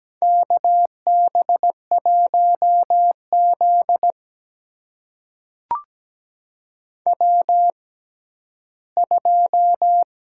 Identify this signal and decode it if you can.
Morse code, heard as a keyed tone, about 17 wpm, spaced as for 11 wpm, 700 Hz. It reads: KB1Z W2
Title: USA Call Signs - 4 Characters - 6x Character Spacing - 17wpm